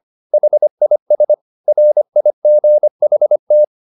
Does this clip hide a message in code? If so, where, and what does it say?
Yes: Morse code from 0.3 to 3.6 s, heard as a keyed tone, about 25 wpm, 600 Hz. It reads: HIS RIGHT